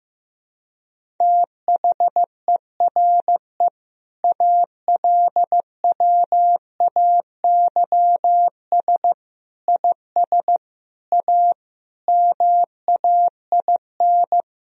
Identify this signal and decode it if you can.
Morse code, heard as a keyed tone, about 15 wpm, 700 Hz. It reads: THERE ALWAYS IS A MAIN
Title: Sentences from Top 1000 Words - Review Part 2 - 15wpm